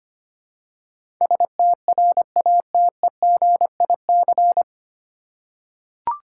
Morse code, heard as a keyed tone, about 25 wpm, 700 Hz. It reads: STRATEGIC
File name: Nine Letter Words - 25wpm